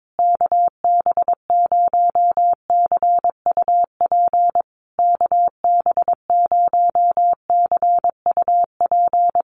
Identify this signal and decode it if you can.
Morse code, heard as a keyed tone, about 22 wpm, 690 Hz. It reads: KB0CUP KB0CUP
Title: USA Call Signs Repeated 3x 22wpm